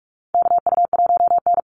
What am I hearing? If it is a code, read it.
Morse code, heard as a keyed tone, about 44 wpm, 700 Hz. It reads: KU1N
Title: Call Signs Repeated 3x - 45wpm